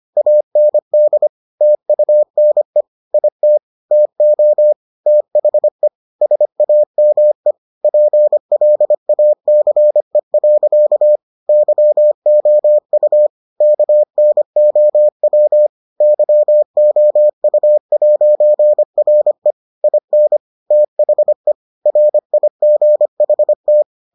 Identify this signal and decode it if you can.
Morse code, heard as a keyed tone, about 25 wpm, 600 Hz. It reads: AND TUNE IT TO THE SAME PLACE. YOU KNOW YOU'RE IN THE RIGHT